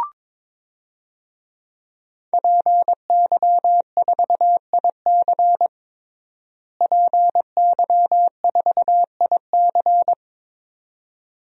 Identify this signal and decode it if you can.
Morse code, heard as a keyed tone, about 22 wpm, 700 Hz. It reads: PY4IC PY4IC
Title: DXCC-100 Call Signs - 5 Characters - Repeated 2x - 22wpm